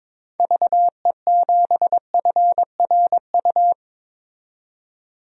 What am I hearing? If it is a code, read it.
Morse code, heard as a keyed tone, about 22 wpm, 700 Hz. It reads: VE7FRU
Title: DXCC-100 Call Signs - 6 Characters - 22wpm